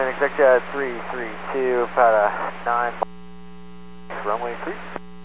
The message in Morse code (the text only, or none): none